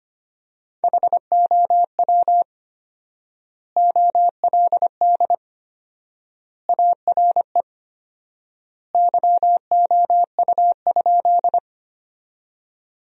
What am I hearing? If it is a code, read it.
Morse code, heard as a keyed tone, about 25 wpm, 700 Hz. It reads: HOW OLD ARE YOU?
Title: 4x Word-Spacing - Sentences from Top 200 Words - 25wpm